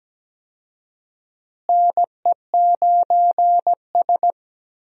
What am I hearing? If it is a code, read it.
Morse code, heard as a keyed tone, about 17 wpm, 700 Hz. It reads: NE9S
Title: USA Call Signs - 4 Characters - Repeated 2x - 17wpm